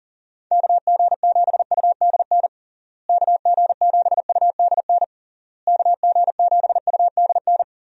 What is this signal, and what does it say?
Morse code, heard as a keyed tone, about 40 wpm, 700 Hz. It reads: KG7UDN KG7UDN KG7UDN
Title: USA Call Signs - 6 Characters - Repeated 3x - 40wpm